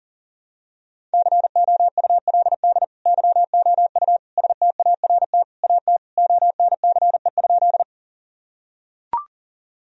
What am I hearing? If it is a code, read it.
Morse code, heard as a keyed tone, about 40 wpm, 700 Hz. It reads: COULD YOU START AT ONCE?
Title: Sentences from Top 1000 Words - Review Part 1 - 40wpm